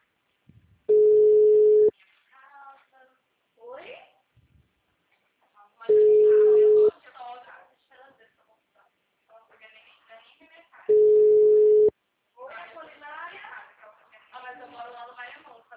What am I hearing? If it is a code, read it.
Ringback tone (Europe, 425 Hz), with speech.
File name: out- -2027- - - .60471